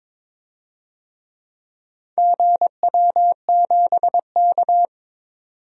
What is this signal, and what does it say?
Morse code, heard as a keyed tone, about 22 wpm, 700 Hz. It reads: GW7K
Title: DXCC-100 Call Signs - 4 Characters - 22wpm